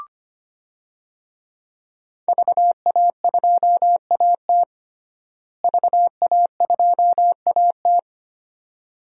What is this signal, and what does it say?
Morse code, heard as a keyed tone, about 25 wpm, 700 Hz. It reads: VA2AT VA2AT
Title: DXCC-100 Call Signs - 5 Characters - Repeated 2x - 25wpm